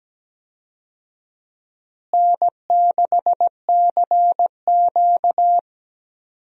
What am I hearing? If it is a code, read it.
Morse code, heard as a keyed tone, about 17 wpm, 700 Hz. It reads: N6CQ
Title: USA Call Signs - 4 Characters - Repeated 2x - 17wpm